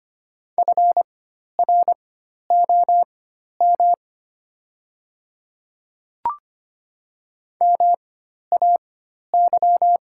Morse code, heard as a keyed tone, about 25 wpm, spaced as for 12 wpm, 700 Hz. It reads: FROM MAY
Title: 4x Character Spacing - Top 200 Words - 25wpm